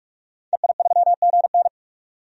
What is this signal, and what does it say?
Morse code, heard as a keyed tone, about 42 wpm, 700 Hz. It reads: EI3GN
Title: DXCC-100 Call Signs - 5 Characters - 45wpm